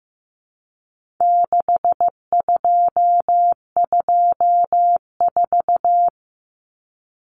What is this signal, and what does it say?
Morse code, heard as a keyed tone, about 15 wpm, 700 Hz. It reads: 6224
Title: Common Cut Numbers - 4-Digit 15wpm